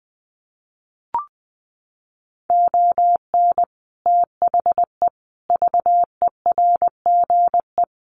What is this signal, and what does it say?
Morse code, heard as a keyed tone, about 20 wpm, 700 Hz. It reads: ON THE VERGE
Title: Sets of 4 Words 20wpm